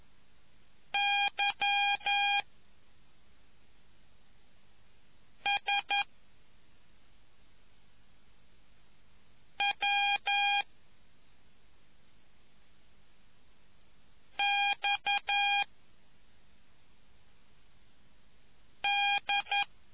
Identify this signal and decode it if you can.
Morse code, heard as a keyed tone, about 11 wpm, 2450 Hz. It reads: Y S W X D